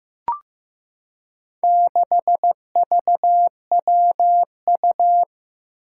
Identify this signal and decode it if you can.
Morse code, heard as a keyed tone, about 15 wpm, 700 Hz. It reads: 6VWU
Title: Four Challenging Letter-Number Combinations 15wpm